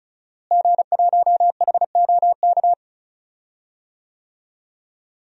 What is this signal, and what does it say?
Morse code, heard as a keyed tone, about 36 wpm, 700 Hz. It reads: G1HOK
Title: DXCC-100 Call Signs - 5 Characters - Repeated 3x - 35wpm